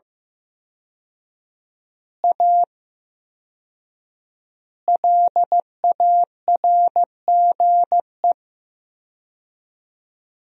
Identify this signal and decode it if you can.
Morse code, heard as a keyed tone, about 15 wpm, 700 Hz. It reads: A LARGE
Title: 4x Word-Spacing - Sentences from Top 500 Words - 15wpm